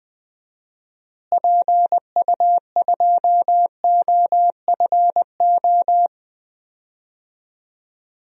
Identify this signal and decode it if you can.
Morse code, heard as a keyed tone, about 20 wpm, 700 Hz. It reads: PU2OFO
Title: DXCC-100 Call Signs - 6 Characters - Repeated 3x - 20wpm